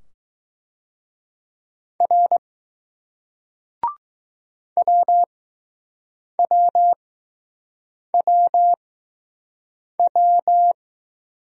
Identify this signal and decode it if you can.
Morse code, heard as a keyed tone, about 20 wpm, spaced as for 6 wpm, 700 Hz. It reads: R WWWW